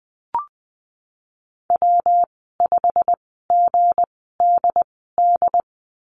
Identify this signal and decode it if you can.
Morse code, heard as a keyed tone, about 20 wpm, spaced as for 15 wpm, 700 Hz. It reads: W5GDD
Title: 161 Lesson 26g - Letter g - Call Signs 20wpm